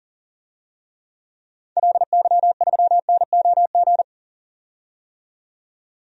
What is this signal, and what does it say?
Morse code, heard as a keyed tone, about 38 wpm, 700 Hz. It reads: LY3NOG
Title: DXCC-100 Call Signs - 6 Characters - 40wpm